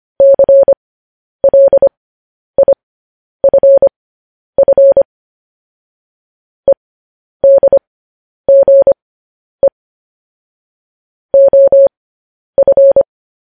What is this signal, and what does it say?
Morse code, heard as a keyed tone, about 25 wpm, spaced as for 10 wpm, 550 Hz. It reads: CLIFF EDGE OF
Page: news headlines in morse code at 10 wpm